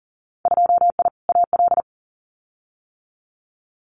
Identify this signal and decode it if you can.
Morse code, heard as a keyed tone, about 39 wpm, 700 Hz. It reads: 2I AL
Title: Winter Field Day Exchanges 40wpm